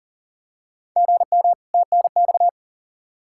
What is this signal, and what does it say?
Morse code, heard as a keyed tone, about 40 wpm, 700 Hz. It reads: GM TNX
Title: Practice QSO - Exchange 1 - Beginning - 40wpm